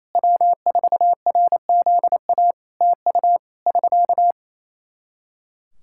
Morse code, heard as a keyed tone, about 28 wpm, 700 Hz. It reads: W4RZA TU <SK>
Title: Practice QSO - Exchange 4 - Ending - 28wpm